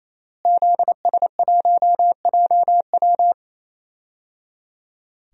Morse code, heard as a keyed tone, about 28 wpm, 700 Hz. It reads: ZS1JW
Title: DXCC-100 Call Signs - 5 Characters - Repeated 3x - 28wpm